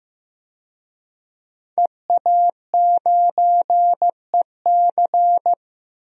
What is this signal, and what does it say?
Morse code, heard as a keyed tone, about 15 wpm, 700 Hz. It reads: EA9EC
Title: DXCC-100 Call Signs - 5 Characters - 15wpm